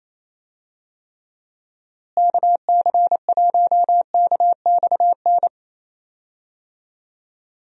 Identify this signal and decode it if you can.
Morse code, heard as a keyed tone, about 27 wpm, 700 Hz. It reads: KC1KXN
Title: USA Call Signs - 6 Characters - Repeated 2x - 28wpm